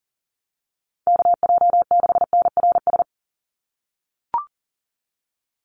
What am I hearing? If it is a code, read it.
Morse code, heard as a keyed tone, about 40 wpm, 700 Hz. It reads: KJ6NRS
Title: USA Call Signs 40wpm